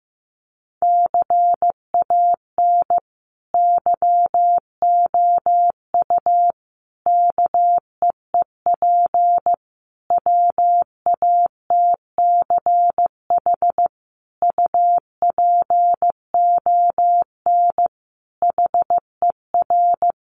Morse code, heard as a keyed tone, about 15 wpm, 700 Hz. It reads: CAN YOU KEEP WATCH UPON HER
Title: Sentences from Top 1100 Words 15wpm